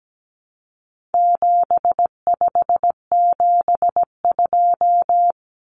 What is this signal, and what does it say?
Morse code, heard as a keyed tone, about 17 wpm, 700 Hz. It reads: 7572
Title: Numbers - 4-Digit 17wpm